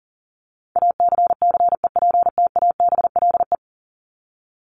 Morse code, heard as a keyed tone, about 40 wpm, 700 Hz. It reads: ACCEPTABLE
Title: Top 1501-1750 Words 40wpm